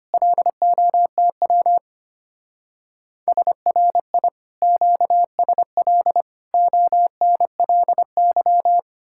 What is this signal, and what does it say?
Morse code, heard as a keyed tone, about 31 wpm, 700 Hz. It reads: LOTW SRI QSL ONLY
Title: Speed-Racing Common QSO Phrases - 25wpm